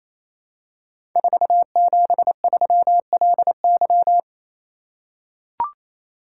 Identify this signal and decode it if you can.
Morse code, heard as a keyed tone, about 27 wpm, 700 Hz. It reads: 473LY
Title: Code Groups - 5 Letter:Numbers - 28wpm